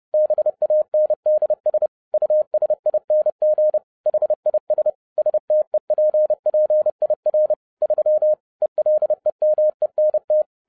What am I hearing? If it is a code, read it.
Morse code, heard as a keyed tone, about 30 wpm, 610 Hz. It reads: BANDS USING HIS STEPPIR 3 ELEMENT